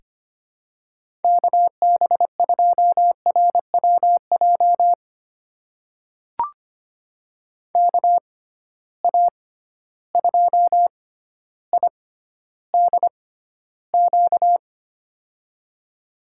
Morse code, heard as a keyed tone, about 25 wpm, spaced as for 13 wpm, 700 Hz. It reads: KB2RWJ KA2IDQ
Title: USA Call Signs - 6 Characters - 6x Character Spacing - 25wpm